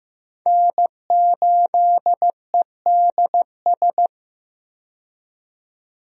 Morse code, heard as a keyed tone, about 15 wpm, 700 Hz. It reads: N8EDS
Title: USA Call Signs - 5 Characters - 15wpm